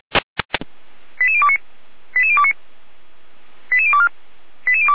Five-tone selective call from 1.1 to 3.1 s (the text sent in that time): 88029 88029